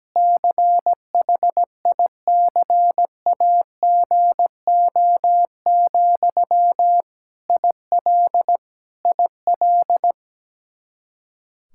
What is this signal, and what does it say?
Morse code, heard as a keyed tone, about 17 wpm, 700 Hz. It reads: CHICAGO, IL IL
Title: Practice QSO - Exchange 1 - Beginning - 17wpm